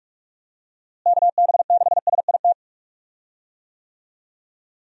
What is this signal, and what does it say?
Morse code, heard as a keyed tone, about 41 wpm, 700 Hz. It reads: KB6SIT
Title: USA Call Signs - 6 Characters - Repeated 3x - 45wpm